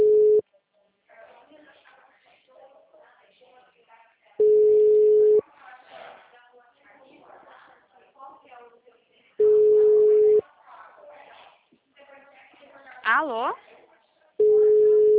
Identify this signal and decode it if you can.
Ringback tone (Europe, 425 Hz), with speech.